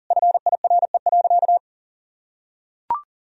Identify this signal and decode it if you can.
Morse code, heard as a keyed tone, about 40 wpm, 700 Hz. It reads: FIRE.
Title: Sentences from Top 1000 Words - Review Part 2 - 40wpm